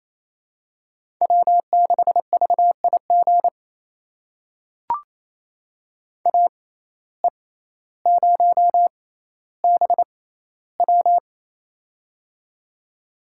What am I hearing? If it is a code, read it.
Morse code, heard as a keyed tone, about 28 wpm, spaced as for 14 wpm, 700 Hz. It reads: W6VIG AE0BW
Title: USA Call Signs - 5 Characters - 6x Character Spacing - 28wpm